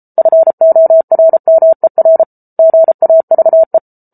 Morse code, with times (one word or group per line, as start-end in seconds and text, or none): 0.2-2.2 FORMER
2.6-3.8 GAVE